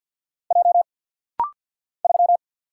Morse code, heard as a keyed tone, about 50 wpm, 700 Hz. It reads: J 3
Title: Single Letter-Number - Rapid-Fire 50wpm